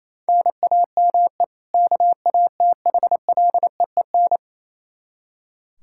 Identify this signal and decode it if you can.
Morse code, heard as a keyed tone, about 27 wpm, 700 Hz. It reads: NAME KATHLEEN
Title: Practice QSO - Exchange 1 - Beginning - 28wpm